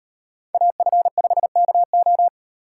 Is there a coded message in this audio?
Yes: Morse code from 0.5 to 2.3 s, heard as a keyed tone, about 38 wpm, 700 Hz. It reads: AF5KO